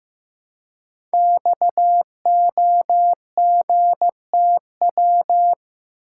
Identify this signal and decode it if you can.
Morse code, heard as a keyed tone, about 15 wpm, 700 Hz. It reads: XOGTW
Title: Code Groups - 5 Letter:Numbers - 15wpm